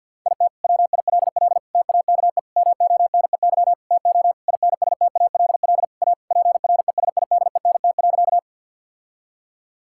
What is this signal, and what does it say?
Morse code, heard as a keyed tone, about 50 wpm, 700 Hz. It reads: IT WILL TAKE MONEY TO INSTALL A PRESIDENT.
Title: Sentences from Top 1600 Words 50wpm